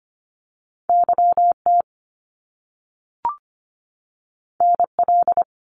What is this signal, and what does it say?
Morse code, heard as a keyed tone, about 25 wpm, 700 Hz. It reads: YT NL
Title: Canadian Provinces and Territories 25wpm